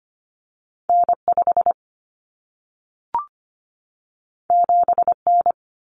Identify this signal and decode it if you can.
Morse code, heard as a keyed tone, about 25 wpm, 700 Hz. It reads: N5 7N
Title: Common Cut Numbers - 2-Digit 25wpm